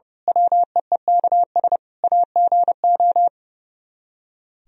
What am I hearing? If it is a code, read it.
Morse code, heard as a keyed tone, about 30 wpm, 700 Hz. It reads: WEEKS AGO
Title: Speed-Racing - Sets of 5 Words 30wpm